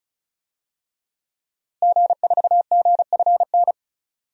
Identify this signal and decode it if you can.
Morse code, heard as a keyed tone, about 36 wpm, 700 Hz. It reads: G4GFN
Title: DXCC-100 Call Signs - 5 Characters - Repeated 3x - 35wpm